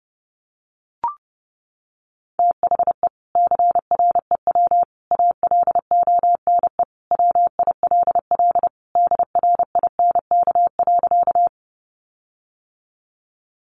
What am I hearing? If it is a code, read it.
Morse code, heard as a keyed tone, about 30 wpm, 700 Hz. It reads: THE CREW ALONE WILL DRINK.
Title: Sentences from Top 1500 Words 30wpm